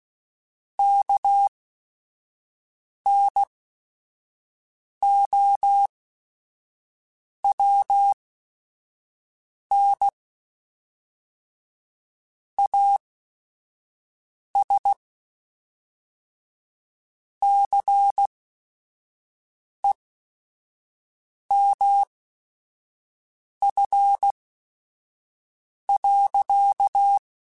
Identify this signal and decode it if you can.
Morse code, heard as a keyed tone, about 16 wpm, spaced as for 5 wpm, 800 Hz. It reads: KNOWN AS CEMF.
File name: Technical Subjects / Technical Subjects-mp3s / XL @5